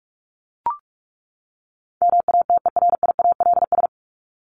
Morse code, heard as a keyed tone, about 44 wpm, 700 Hz. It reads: MATERIALS
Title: ES:S Suffix - Top 100 Words with Base Words 45wpm